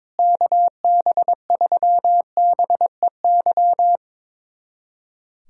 Morse code, heard as a keyed tone, about 22 wpm, 700 Hz. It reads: KB3BEY
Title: USA Callsign DE Callsign - 6 Characters - 22wpm